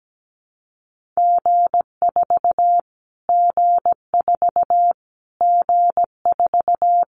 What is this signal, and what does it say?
Morse code, heard as a keyed tone, about 17 wpm, 700 Hz. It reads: G4 G4 G4
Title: Callsign Prefixes for Top 25 DXCC - Repeated 3x - 17wpm